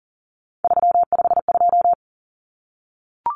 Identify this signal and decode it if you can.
Morse code, heard as a keyed tone, about 40 wpm, 700 Hz. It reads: 352
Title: Numbers - 3-Digit 40wpm